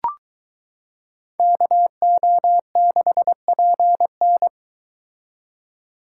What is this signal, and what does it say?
Morse code, heard as a keyed tone, about 21 wpm, 700 Hz. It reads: KO6PN